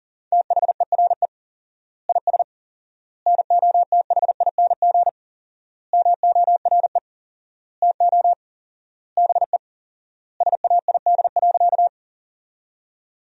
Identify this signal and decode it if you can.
Morse code, heard as a keyed tone, about 40 wpm, 700 Hz. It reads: THERE IS NOTHING MORE TO BE SAID.
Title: 4x Word-Spacing - Sentences from Top 500 Words - 40wpm